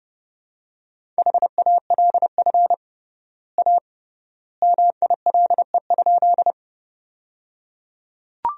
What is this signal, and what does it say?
Morse code, heard as a keyed tone, about 30 wpm, 700 Hz. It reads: HALF A MILE?
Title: 3x Word-Spacing - Sentences from Top 400 Words - 30wpm